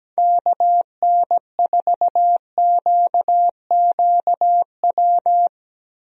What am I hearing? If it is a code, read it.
Morse code, heard as a keyed tone, about 17 wpm, 700 Hz. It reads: KN4QQW
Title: USA Call Signs - 6 Characters - Repeated 2x - 17wpm